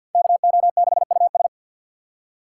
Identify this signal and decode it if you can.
Morse code, heard as a keyed tone, about 46 wpm, 700 Hz. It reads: KO6US
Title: USA Call Signs - 4x Character Spacing - 50wpm